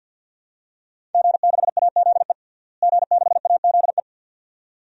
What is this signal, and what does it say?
Morse code, heard as a keyed tone, about 51 wpm, 700 Hz. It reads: G6AZE G6AZE
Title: DXCC-100 Call Signs - 5 Characters - Repeated 2x - 50wpm